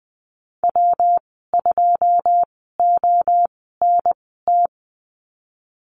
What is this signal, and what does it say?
Morse code, heard as a keyed tone, about 20 wpm, spaced as for 15 wpm, 700 Hz. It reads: W2ONT
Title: 128 Lesson 21g - Number 6 - Call Signs 20wpm